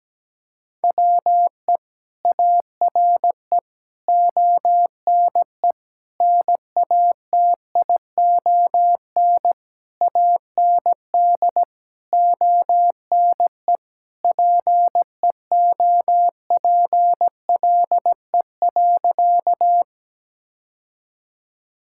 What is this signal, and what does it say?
Morse code, heard as a keyed tone, about 17 wpm, 700 Hz. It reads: WE ARE ONE NATION AND ONE PEOPLE.
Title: Sentences from Top 1000 Words - Review Part 6 - 17wpm